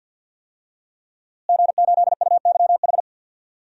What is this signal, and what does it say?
Morse code, heard as a keyed tone, about 51 wpm, 700 Hz. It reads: G8UYH
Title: DXCC-100 Call Signs - 5 Characters - Repeated 3x - 50wpm